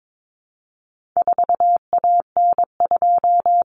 Morse code, heard as a keyed tone, about 22 wpm, 700 Hz. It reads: 4AN2
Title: Common Cut Numbers - 4-Digit 22wpm